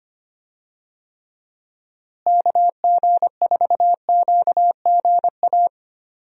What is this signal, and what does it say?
Morse code, heard as a keyed tone, about 25 wpm, 700 Hz. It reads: KG4QGA
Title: USA Call Signs - 6 Characters - Repeated 3x - 25wpm